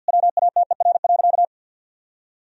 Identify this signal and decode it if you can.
Morse code, heard as a keyed tone, about 51 wpm, 700 Hz. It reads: WATER.
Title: Sentences from Top 1000 Words - Review Part 2 - 50wpm